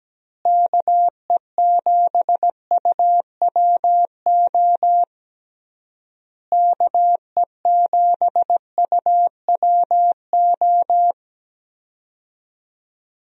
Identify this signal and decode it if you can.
Morse code, heard as a keyed tone, about 17 wpm, 700 Hz. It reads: KE7UWO KE7UWO
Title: USA Call Signs - 6 Characters - Repeated 3x - 17wpm